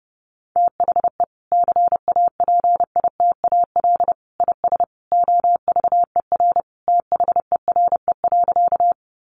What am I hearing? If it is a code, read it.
Morse code, heard as a keyed tone, about 30 wpm, 700 Hz. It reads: THE CAPITAL IS OVER THERE.
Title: Sentences from Top 1000 Words 30wpm